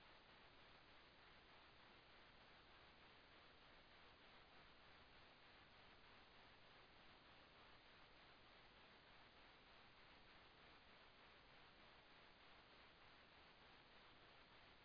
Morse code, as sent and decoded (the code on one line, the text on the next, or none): none